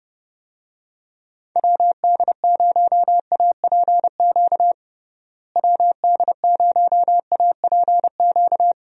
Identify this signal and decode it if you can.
Morse code, heard as a keyed tone, about 30 wpm, 700 Hz. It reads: WD0APQ WD0APQ
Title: USA Call Signs - 6 Characters - Repeated 3x - 30wpm